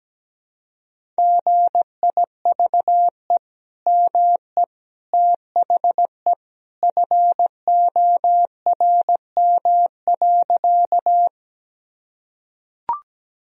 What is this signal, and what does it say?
Morse code, heard as a keyed tone, about 17 wpm, 700 Hz. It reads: GIVE ME THE FORM.
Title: Sentences from Top 1000 Words - Review Part 1 - 17wpm